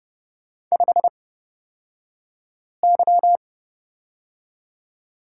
Morse code, heard as a keyed tone, about 30 wpm, spaced as for 5 wpm, 700 Hz. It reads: 5Y
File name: Lesson 37d - Letter x - Two Characters 30wpm